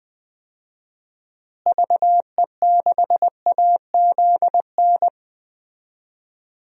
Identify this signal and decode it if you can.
Morse code, heard as a keyed tone, about 20 wpm, 700 Hz. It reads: VE6AZN
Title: DXCC-100 Call Signs - 6 Characters - Repeated 3x - 20wpm